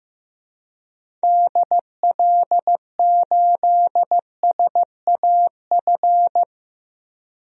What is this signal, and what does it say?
Morse code, heard as a keyed tone, about 15 wpm, 700 Hz. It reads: DL8SAF
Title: DXCC-100 Call Signs - 6 Characters - Repeated 3x - 15wpm